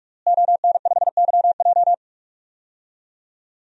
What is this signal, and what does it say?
Morse code, heard as a keyed tone, about 42 wpm, 700 Hz. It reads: ON5YJ
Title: DXCC-100 Call Signs - 5 Characters - Repeated 3x - 45wpm